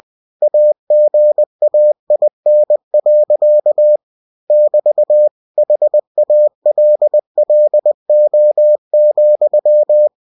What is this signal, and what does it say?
Morse code, heard as a keyed tone, about 20 wpm, 600 Hz. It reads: AGAIN. <BT> HALLO,
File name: cw-books / winnie-the-pooh / winnie-20wpm-page-0106